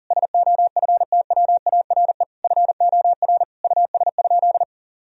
Morse code, heard as a keyed tone, about 40 wpm, 700 Hz. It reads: SOFTWARE FOR US?
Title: Sentences from Top 1000 Words - Review Part 3 - 40wpm